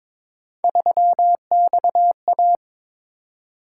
Morse code, heard as a keyed tone, about 22 wpm, 700 Hz. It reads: 3XA
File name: practice / 22wpm / Three Letter-Number Combinations - 22wpm